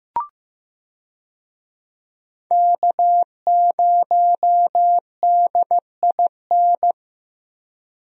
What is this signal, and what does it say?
Morse code, heard as a keyed tone, about 15 wpm, 700 Hz. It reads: K0DIN
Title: USA Call Signs - 5 Characters - Repeated 3x - 15wpm